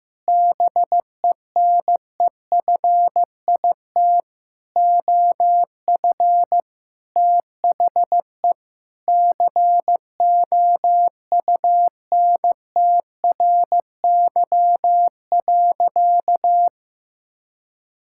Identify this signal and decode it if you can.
Morse code, heard as a keyed tone, about 16 wpm, 700 Hz. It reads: BENEFIT OF THE COUNTRY.
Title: Sentences from Top 1000 Words - Review Part 6 - 15wpm